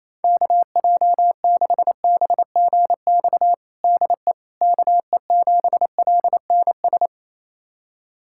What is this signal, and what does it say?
Morse code, heard as a keyed tone, about 28 wpm, 700 Hz. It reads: KJ6BGX DE KE7LNS
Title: USA Callsign DE Callsign - 6 Characters - 28wpm